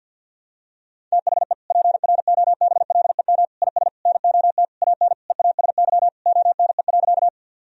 Morse code, heard as a keyed tone, about 52 wpm, 700 Hz. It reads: THE PROBLEM IS NOT AN EASY ONE.
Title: Sentences from Top 1000 Words - Review Part 2 - 50wpm